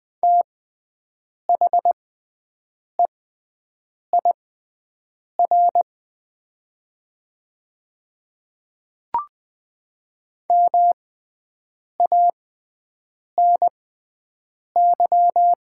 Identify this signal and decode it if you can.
Morse code, heard as a keyed tone, about 20 wpm, spaced as for 7 wpm, 700 Hz. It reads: THEIR MANY
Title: 6x Character Spacing - Top 100 Words - 20wpm